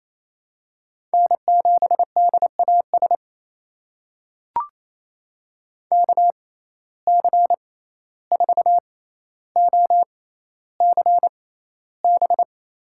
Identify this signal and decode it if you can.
Morse code, heard as a keyed tone, about 28 wpm, spaced as for 14 wpm, 700 Hz. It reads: N7DAS KC4OCB
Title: USA Call Signs - 6x Character Spacing - 28wpm